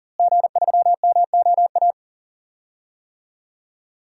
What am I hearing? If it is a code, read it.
Morse code, heard as a keyed tone, about 38 wpm, 700 Hz. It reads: G3MOA
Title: DXCC-100 Call Signs - 5 Characters - Repeated 3x - 40wpm